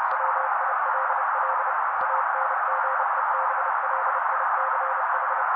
Morse code, heard as a keyed tone, about 30 wpm, 530 Hz. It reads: WILL ANGEBLICH
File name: daily / 29wpm noise